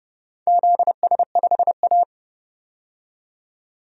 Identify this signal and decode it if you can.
Morse code, heard as a keyed tone, about 30 wpm, 700 Hz. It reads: ZS5A
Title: DXCC-100 Call Signs - 4 Characters - 30wpm